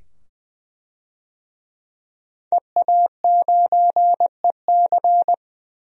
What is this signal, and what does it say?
Morse code, heard as a keyed tone, about 20 wpm, 700 Hz. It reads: EA9EC
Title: DXCC-100 Call Signs - 5 Characters - 20wpm